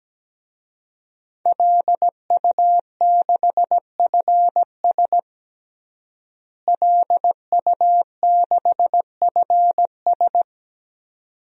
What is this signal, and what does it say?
Morse code, heard as a keyed tone, about 17 wpm, 700 Hz. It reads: LU6FS LU6FS
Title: DXCC-100 Call Signs - 5 Characters - Repeated 2x - 17wpm